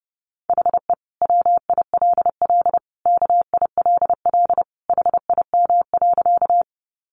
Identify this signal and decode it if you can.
Morse code, heard as a keyed tone, about 30 wpm, 700 Hz. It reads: HE WILL KILL HIM.
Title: Sentences from Top 1400 Words 30wpm